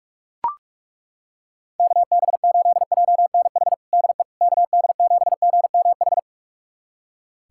Morse code, heard as a keyed tone, about 42 wpm, 700 Hz. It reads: KB8JNH DE KD7GMH
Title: USA Callsign DE Callsign - 6 Characters - 45wpm